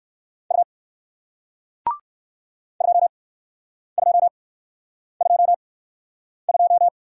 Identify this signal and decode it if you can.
Morse code, heard as a keyed tone, about 58 wpm, spaced as for 9 wpm, 700 Hz. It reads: U 2222